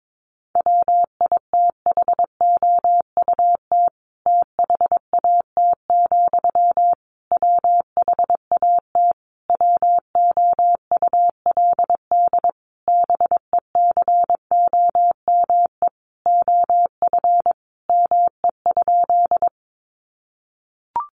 Morse code, heard as a keyed tone, about 22 wpm, 700 Hz. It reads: WITHOUT THAT, WHAT WOULD BECOME OF ME?
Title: Sentences from Top 600 Words 22wpm